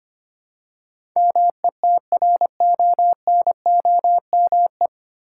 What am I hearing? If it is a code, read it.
Morse code, heard as a keyed tone, about 25 wpm, 700 Hz. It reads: METRONOME
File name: practice / 25wpm / Difficult Words with Hints - 25wpm